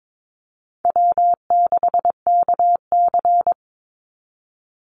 Morse code, heard as a keyed tone, about 22 wpm, 700 Hz. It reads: W6KC
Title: Call Signs - 22wpm